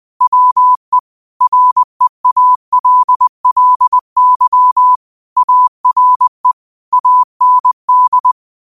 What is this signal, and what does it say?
Morse code, heard as a keyed tone, about 20 wpm, 1000 Hz. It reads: WE REALLY ARE AND
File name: qotdfiles / 20wpm